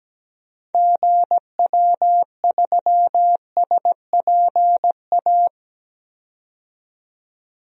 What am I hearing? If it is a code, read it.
Morse code, heard as a keyed tone, about 17 wpm, 700 Hz. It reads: GW3SPA
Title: DXCC-100 Call Signs - 6 Characters - 17wpm